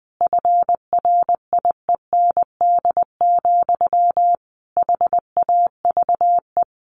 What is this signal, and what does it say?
Morse code, heard as a keyed tone, about 19 wpm, 700 Hz. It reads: FRIEND, HAVE
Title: A Sound Of Thunder 20wpm